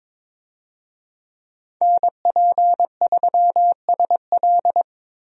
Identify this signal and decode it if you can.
Morse code, heard as a keyed tone, about 21 wpm, 700 Hz. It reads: NP3SL
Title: USA Call Signs - 5 Characters - 22wpm